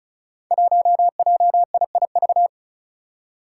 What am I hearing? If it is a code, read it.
Morse code, heard as a keyed tone, about 36 wpm, 700 Hz. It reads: 1JIIV
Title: Five Challenging Letter-Number Combinations 35wpm